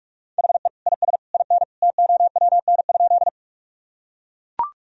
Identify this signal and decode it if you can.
Morse code, heard as a keyed tone, about 42 wpm, 700 Hz. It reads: HE IS IN TOWN?
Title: Sentences from Top 1000 Words - Review Part 2 - 45wpm